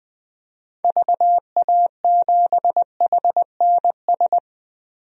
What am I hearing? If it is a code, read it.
Morse code, heard as a keyed tone, about 20 wpm, 700 Hz. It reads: VA7HNS
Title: DXCC-100 Call Signs - 6 Characters - 20wpm